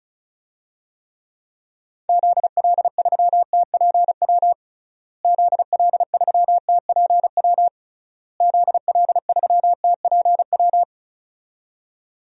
Morse code, heard as a keyed tone, about 36 wpm, 700 Hz. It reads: ZL3TPW ZL3TPW ZL3TPW
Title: DXCC-100 Call Signs - 6 Characters - Repeated 3x - 35wpm